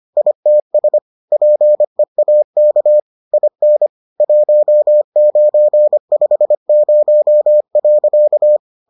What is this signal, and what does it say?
Morse code, heard as a keyed tone, about 25 wpm, 600 Hz. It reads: ITS PEAK IN 1950.